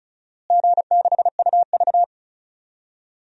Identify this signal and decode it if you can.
Morse code, heard as a keyed tone, about 34 wpm, 700 Hz. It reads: G6UV
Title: DXCC-100 Call Signs - 4 Characters - Repeated 2x - 35wpm